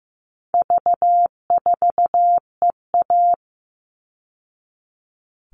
Morse code, heard as a keyed tone, about 15 wpm, 700 Hz. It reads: V4EA
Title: All Cut Numbers - 4-Digit 15wpm